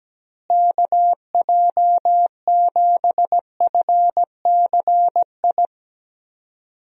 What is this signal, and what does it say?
Morse code, heard as a keyed tone, about 17 wpm, 700 Hz. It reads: KJ7FCI
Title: USA Call Signs - 6 Characters - Repeated 3x - 17wpm